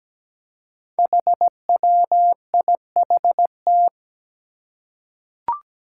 Morse code, heard as a keyed tone, about 17 wpm, 700 Hz. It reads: HWIHT
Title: Five Challenging Letter-Number Combinations 17wpm